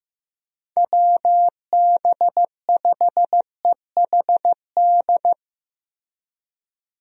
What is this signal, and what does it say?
Morse code, heard as a keyed tone, about 15 wpm, 700 Hz. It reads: WB5EHD
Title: USA Call Signs - 6 Characters - Repeated 2x - 15wpm